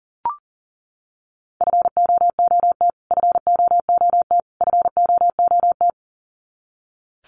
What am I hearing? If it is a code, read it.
Morse code, heard as a keyed tone, about 38 wpm, 700 Hz. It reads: FOOT FOOT FOOT
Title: Top 500 Words Repeated 40wpm